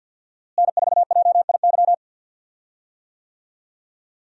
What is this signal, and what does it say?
Morse code, heard as a keyed tone, about 52 wpm, 700 Hz. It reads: N4JIY
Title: USA Call Signs - 5 Characters - Repeated 3x - 50wpm